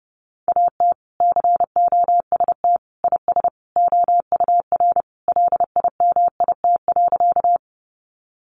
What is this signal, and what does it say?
Morse code, heard as a keyed tone, about 30 wpm, 700 Hz. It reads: AT COST IS OUR LIMIT.
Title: Sentences from Top 1500 Words 30wpm